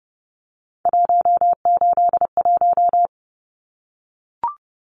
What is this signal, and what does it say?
Morse code, heard as a keyed tone, about 30 wpm, 700 Hz. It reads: 181
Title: Numbers - 3-Digit 30wpm